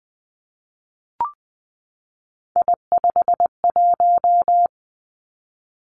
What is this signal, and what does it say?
Morse code, heard as a keyed tone, about 20 wpm, 700 Hz. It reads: I51
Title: Three Challenging Letter-Number Combinations 20wpm